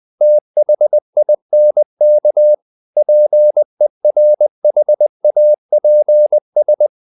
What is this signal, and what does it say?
Morse code, heard as a keyed tone, about 20 wpm, 600 Hz. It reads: THINK PERHAPS
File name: the old man and the sea / the old man and the sea part 1 20wpm